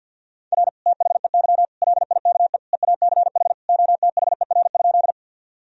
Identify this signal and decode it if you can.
Morse code, heard as a keyed tone, about 48 wpm, 700 Hz. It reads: R THEY LIKE EACH OTHER?
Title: Sentences from Top 200 Words and Abbreviations 50wpm